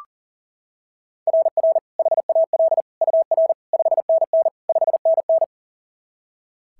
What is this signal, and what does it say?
Morse code, heard as a keyed tone, about 37 wpm, 690 Hz. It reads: RR HAL UR 5NN 5NN
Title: SOTA QSO Practice 40wpm